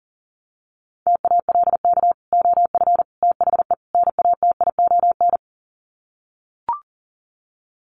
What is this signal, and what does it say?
Morse code, heard as a keyed tone, about 40 wpm, 700 Hz. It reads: TALK OF THE NATION
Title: Sets of 4 Words 40wpm